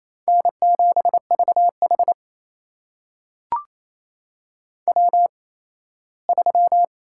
Morse code, heard as a keyed tone, about 27 wpm, spaced as for 17 wpm, 700 Hz. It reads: N7VH W3
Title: USA Call Signs - 4 Characters - 8x Character Spacing - 28wpm